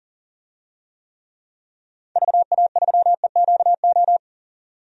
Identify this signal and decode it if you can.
Morse code, heard as a keyed tone, about 40 wpm, 700 Hz. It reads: VA3EQO
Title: DXCC-100 Call Signs - 6 Characters - 40wpm